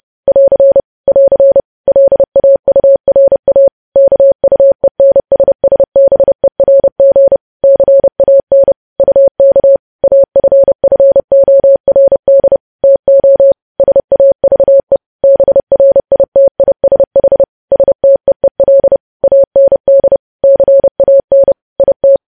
Morse code, heard as a keyed tone, about 30 wpm, 550 Hz. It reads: <AR> <AR> LAURA KUENSSBERG CAN UK AFFORD TO SAVE BRITISH STEEL AND CAN IT
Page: news headlines in morse code at 30 wpm